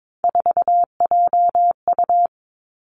Morse code, heard as a keyed tone, about 22 wpm, 700 Hz. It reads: 4JU
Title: Three Challenging Letter-Number Combinations 22wpm